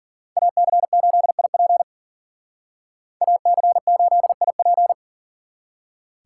Morse code, heard as a keyed tone, about 51 wpm, 700 Hz. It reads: AC8IP AC8IP